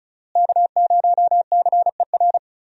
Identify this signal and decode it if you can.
Morse code, heard as a keyed tone, about 34 wpm, 700 Hz. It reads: K0CER
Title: USA Call Signs - 5 Characters - Repeated 2x - 35wpm